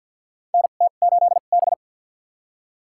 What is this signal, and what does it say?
Morse code, heard as a keyed tone, about 52 wpm, spaced as for 36 wpm, 700 Hz. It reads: NT8B
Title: USA Call Signs - 2x Character Spacing - 50wpm